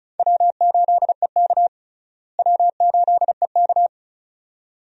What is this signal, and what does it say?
Morse code, heard as a keyed tone, about 34 wpm, 700 Hz. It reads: W8EK W8EK
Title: USA Call Signs - 4 Characters - Repeated 2x - 35wpm